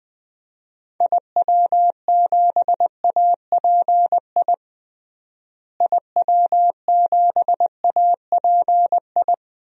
Morse code, heard as a keyed tone, about 20 wpm, 700 Hz. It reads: IW7API IW7API
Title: DXCC-100 Call Signs - 6 Characters - Repeated 3x - 20wpm